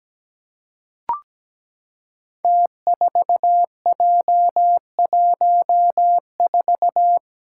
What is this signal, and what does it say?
Morse code, heard as a keyed tone, about 17 wpm, 700 Hz. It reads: T4J14
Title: Five Challenging Letter-Number Combinations 17wpm